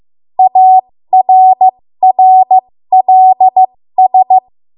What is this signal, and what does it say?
Morse code, heard as a keyed tone, about 15 wpm, spaced as for 13 wpm, 750 Hz. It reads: ARRLS
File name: Morse / 13WPM